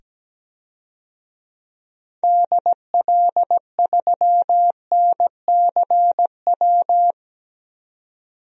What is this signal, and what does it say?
Morse code, heard as a keyed tone, about 17 wpm, 700 Hz. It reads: DL3NCW
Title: DXCC-100 Call Signs - 6 Characters - 17wpm